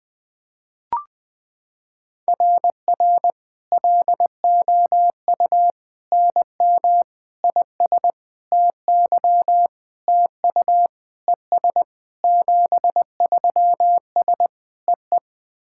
Morse code, heard as a keyed tone, about 21 wpm, 700 Hz. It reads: RR LOU NM IS TY TU ES 73S EE
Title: SOTA QSO Practice 20wpm